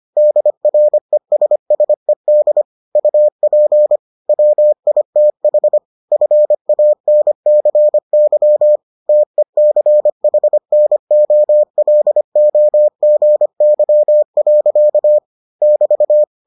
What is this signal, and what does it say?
Morse code, heard as a keyed tone, about 25 wpm, 600 Hz. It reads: DRESSED UP WITH FANCY TECHNOLOGY. <BT>